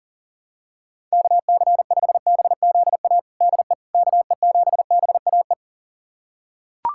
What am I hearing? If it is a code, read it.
Morse code, heard as a keyed tone, about 40 wpm, 700 Hz. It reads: KC5BZA DE KE7BAE
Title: USA Callsign DE Callsign 40wpm